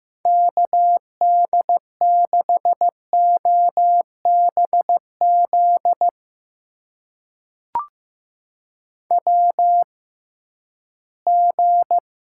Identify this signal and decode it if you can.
Morse code, heard as a keyed tone, about 15 wpm, spaced as for 11 wpm, 700 Hz. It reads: KD6OBZ WG
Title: USA Call Signs - 6x Character Spacing - 15wpm